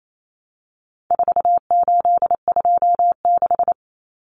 Morse code, heard as a keyed tone, about 28 wpm, 700 Hz. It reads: 4826
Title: Numbers - 4-Digit 28wpm